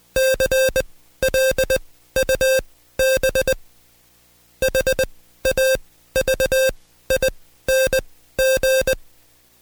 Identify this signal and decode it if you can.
Morse code, heard as a keyed tone, about 20 wpm, spaced as for 13 wpm, 530 Hz. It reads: CLUB HAVING
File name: Code Quick / AudioDownld2 / Seg133-Stepper6 13WPM